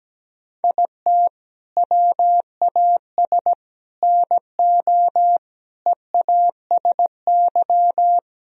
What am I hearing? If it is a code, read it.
Morse code, heard as a keyed tone, about 17 wpm, 700 Hz. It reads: IT WAS NO EASY
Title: Sentences from Top 1000 Words - Review Part 3 - 17wpm